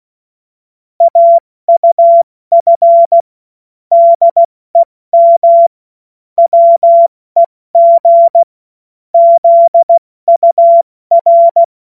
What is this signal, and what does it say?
Morse code, heard as a keyed tone, about 16 wpm, 680 Hz. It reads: AUF DEM WEG ZUR
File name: Morsekurs / Witze / Witz 05. April 2022 70 F80